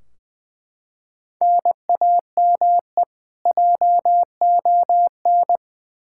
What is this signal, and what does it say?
Morse code, heard as a keyed tone, about 20 wpm, 700 Hz. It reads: NAME JON
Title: K1USN SST QSO Practice 20wpm